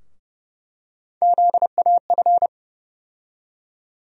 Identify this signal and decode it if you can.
Morse code, heard as a keyed tone, about 30 wpm, 700 Hz. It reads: ZAF
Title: USA Call Sign Postfixes - 3 Letters 30wpm